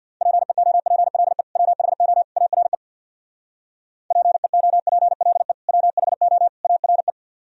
Morse code, heard as a keyed tone, about 55 wpm, 700 Hz. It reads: PEOPLE WHO ARE PEOPLE WHO ARE
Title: Speed-Racing - Sets of 3 Words 50wpm